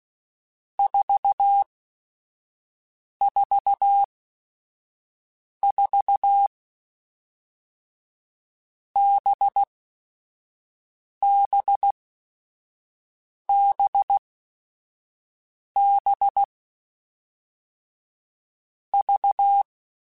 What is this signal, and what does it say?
Morse code, heard as a keyed tone, about 16 wpm, 800 Hz. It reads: 4 4 4 B B B B V
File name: Groups@5 / Groups@5 mp3s / VB46@5